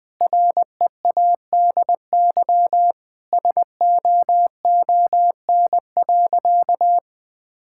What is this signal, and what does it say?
Morse code, heard as a keyed tone, about 20 wpm, 700 Hz. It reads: READY SOON.
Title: Sentences with 2-Word Phrases - 20wpm